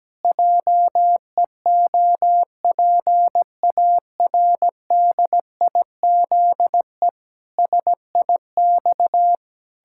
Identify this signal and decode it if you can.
Morse code, heard as a keyed tone, about 17 wpm, 700 Hz. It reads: JEOPARDIZE SIX
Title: Pangram Challenge v2 - 17wpm